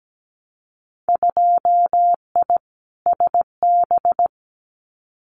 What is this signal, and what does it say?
Morse code, heard as a keyed tone, about 17 wpm, 700 Hz. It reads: 2I SB
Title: Winter Field Day Exchanges Encore 17wpm